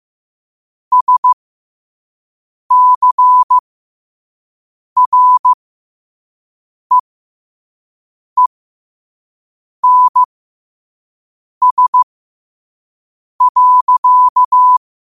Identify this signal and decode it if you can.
Morse code, heard as a keyed tone, about 15 wpm, spaced as for 5 wpm, 1000 Hz. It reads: SCREENS.
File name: qotdfiles / 5wpm